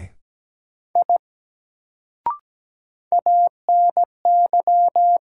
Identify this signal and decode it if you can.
Morse code, heard as a keyed tone, about 17 wpm, 700 Hz. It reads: I ANY
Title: Sentences from Top 200 Words - New Word Drill - 17wpm